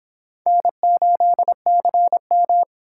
Morse code, heard as a keyed tone, about 26 wpm, 700 Hz. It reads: N8CM